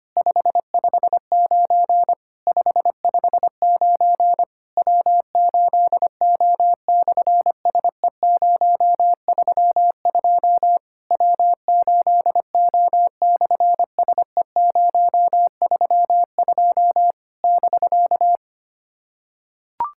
Morse code, heard as a keyed tone, about 24 wpm, 700 Hz. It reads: 559 559 W8O/SE032 W8O/SE032 <BK>
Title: SOTA QSO Practice 25wpm